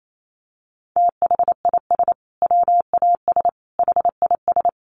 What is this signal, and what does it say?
Morse code, heard as a keyed tone, about 28 wpm, 700 Hz. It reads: THIS WAS HIS
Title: Sentences from Top 1500 Words 28wpm